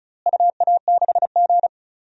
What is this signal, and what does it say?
Morse code, heard as a keyed tone, about 36 wpm, 700 Hz. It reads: UA6G
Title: DXCC-100 Call Signs - 4 Characters - Repeated 3x - 35wpm